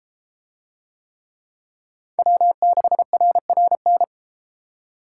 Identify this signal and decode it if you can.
Morse code, heard as a keyed tone, about 31 wpm, 700 Hz. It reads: W6RRN